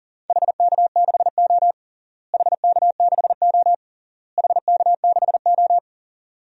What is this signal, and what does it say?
Morse code, heard as a keyed tone, about 40 wpm, 700 Hz. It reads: HK6O HK6O HK6O
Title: DXCC-100 Call Signs - 4 Characters - Repeated 3x - 40wpm